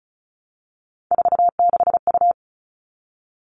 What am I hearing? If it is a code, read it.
Morse code, heard as a keyed tone, about 34 wpm, 700 Hz. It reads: 46U
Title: All Cut Numbers - 3-Digit 35wpm